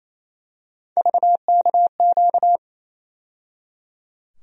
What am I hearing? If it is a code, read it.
Morse code, heard as a keyed tone, about 28 wpm, 700 Hz. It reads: VKQ
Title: USA Call Sign Postfixes - 3 Letters 28wpm